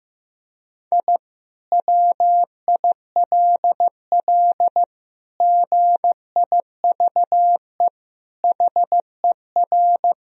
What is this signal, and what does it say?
Morse code, heard as a keyed tone, about 15 wpm, 700 Hz. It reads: I WILL GIVE HER
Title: Sentences from Top 1000 Words - Review Part 2 - 15wpm